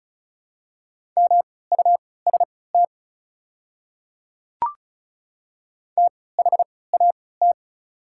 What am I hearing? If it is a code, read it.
Morse code, heard as a keyed tone, about 36 wpm, spaced as for 20 wpm, 700 Hz. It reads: MUST THAT
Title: 3x Character Spacing - Top 300 Words - 35wpm